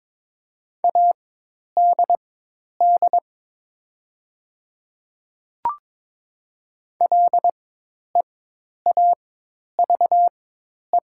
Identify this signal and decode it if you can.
Morse code, heard as a keyed tone, about 22 wpm, spaced as for 10 wpm, 690 Hz. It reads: ADD LEAVE
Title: 4x Character Spacing - Top 500 Words - 22wpm